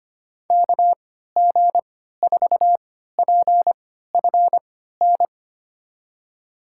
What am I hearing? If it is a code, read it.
Morse code, heard as a keyed tone, about 25 wpm, spaced as for 14 wpm, 700 Hz. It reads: KG4PFN
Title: USA Call Signs - 6 Characters - 3x Character Spacing - 25wpm